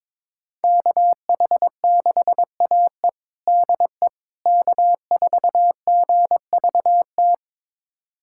Morse code, heard as a keyed tone, about 23 wpm, 700 Hz. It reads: KH6AE DE K4GVT
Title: USA Callsign DE Callsign - 5 Characters - 22wpm